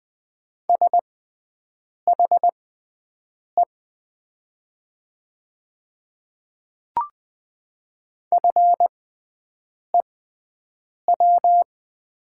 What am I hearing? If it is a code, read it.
Morse code, heard as a keyed tone, about 20 wpm, spaced as for 7 wpm, 700 Hz. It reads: SHE FEW
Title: 6x Character Spacing - Top 500 Words - 20wpm